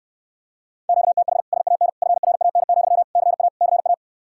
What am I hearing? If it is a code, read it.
Morse code, heard as a keyed tone, about 74 wpm, 700 Hz. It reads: QTH SAN FRAN, CA CA